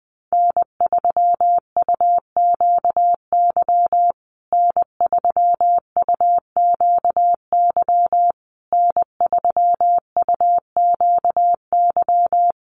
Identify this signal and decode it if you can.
Morse code, heard as a keyed tone, about 19 wpm, 700 Hz. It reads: N3UQY N3UQY N3UQY
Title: USA Call Signs Repeated 3x 20wpm